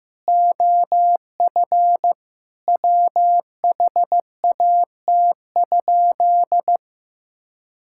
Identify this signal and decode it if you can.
Morse code, heard as a keyed tone, about 14 wpm, 700 Hz. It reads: OF WHAT?
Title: Sentences from Top 1000 Words - Review Part 5 - 15wpm